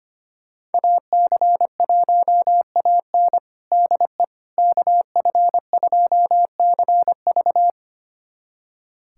Morse code, heard as a keyed tone, about 25 wpm, 700 Hz. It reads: AC1AN DE KF2CV
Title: USA Callsign DE Callsign - 5 Characters - 25wpm